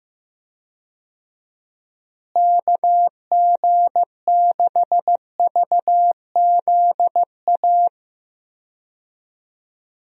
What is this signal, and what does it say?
Morse code, heard as a keyed tone, about 16 wpm, 700 Hz. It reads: KG6VZA
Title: USA Call Signs - 6 Characters - Repeated 3x - 15wpm